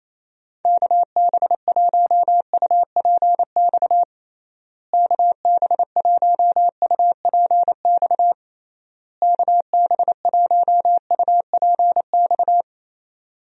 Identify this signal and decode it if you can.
Morse code, heard as a keyed tone, about 28 wpm, 700 Hz. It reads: KB1UPX KB1UPX KB1UPX
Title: USA Call Signs - 6 Characters - Repeated 3x - 28wpm